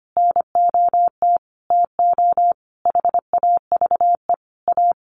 Morse code, heard as a keyed tone, about 25 wpm, 700 Hz. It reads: NOT TO HAVE A
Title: Sentences from Top 1500 Words 25wpm